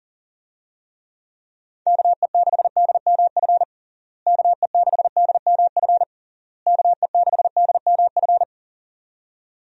Morse code, heard as a keyed tone, about 37 wpm, 700 Hz. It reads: KE6DMF KE6DMF KE6DMF
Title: USA Call Signs - 6 Characters - Repeated 3x - 40wpm